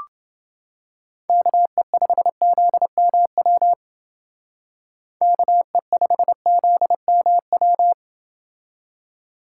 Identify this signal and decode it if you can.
Morse code, heard as a keyed tone, about 29 wpm, 700 Hz. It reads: KE5ZMW KE5ZMW